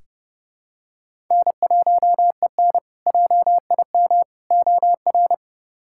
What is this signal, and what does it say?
Morse code, heard as a keyed tone, about 31 wpm, 700 Hz. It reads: N1EN JIM OR
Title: K1USN SST QSO Practice - 30wpm